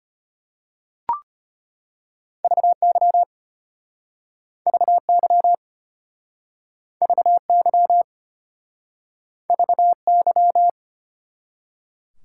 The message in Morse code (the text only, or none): VY VY VY VY